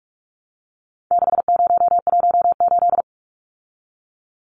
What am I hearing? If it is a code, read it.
Morse code, heard as a keyed tone, about 44 wpm, 700 Hz. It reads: 6018
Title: Numbers - 4-Digit 45wpm